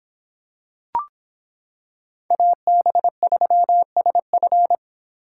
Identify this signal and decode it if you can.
Morse code, heard as a keyed tone, about 26 wpm, 700 Hz. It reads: AB3SF